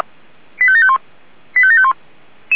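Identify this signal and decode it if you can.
Five-tone selective call (ZVEI), heard as quick runs of five tones, tones of about 70 ms. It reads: 85751 85751
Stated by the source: ZVEI five-tone selective call received via SDR.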